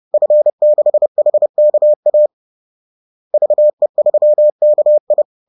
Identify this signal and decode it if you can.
Morse code, heard as a keyed tone, about 30 wpm, 600 Hz. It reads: F6HKA VE3KI
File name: 30wpm-4